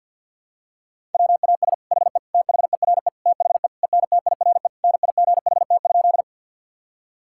Morse code, heard as a keyed tone, about 52 wpm, 700 Hz. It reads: WAS HE THERE THE ENTIRE NIGHT?
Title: Sentences from Top 1000 Words - Review Part 5 - 50wpm